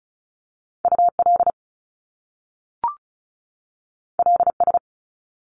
Morse code, heard as a keyed tone, about 36 wpm, 700 Hz. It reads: UL LS
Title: Top Two Letter Combinations in Words 35wpm